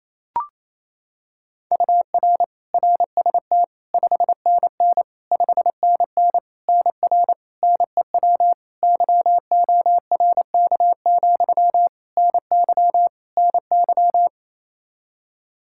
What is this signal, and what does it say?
Morse code, heard as a keyed tone, about 28 wpm, 700 Hz. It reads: UR RST 5NN 5NN NR NEW YORK, NY NY
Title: Practice QSO - Exchange 1 - Beginning - 28wpm